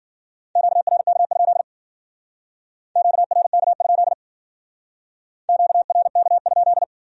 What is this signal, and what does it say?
Morse code, heard as a keyed tone, about 49 wpm, 700 Hz. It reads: QRK? QRK? QRK?
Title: Speed-Racing Q-Codes - 40wpm